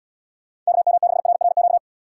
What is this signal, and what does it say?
Morse code, heard as a keyed tone, about 69 wpm, 700 Hz. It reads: KM6RRQ